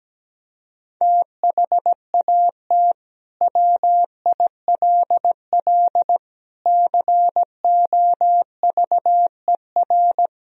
Morse code, heard as a keyed tone, about 17 wpm, 700 Hz. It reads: THAT WILL COVER
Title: Sentences from Top 1000 Words - Review Part 1 - 17wpm